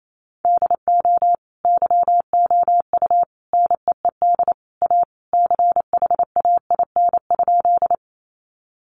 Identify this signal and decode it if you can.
Morse code, heard as a keyed tone, about 28 wpm, 700 Hz. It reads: DO YOU NEED A CHAIN?
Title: Sentences from Top 1500 Words 28wpm